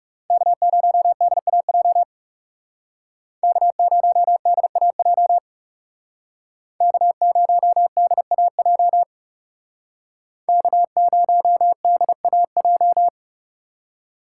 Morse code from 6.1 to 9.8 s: K0DAJ

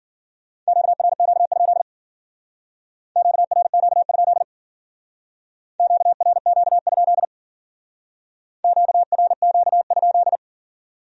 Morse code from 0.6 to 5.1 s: QRQ? QRQ?

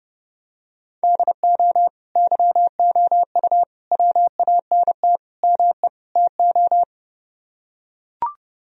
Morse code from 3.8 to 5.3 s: WANT

